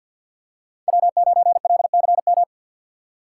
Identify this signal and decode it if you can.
Morse code, heard as a keyed tone, about 50 wpm, 690 Hz. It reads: W9LCM
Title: USA Call Signs - 5 Characters - 4x Character Spacing - 50wpm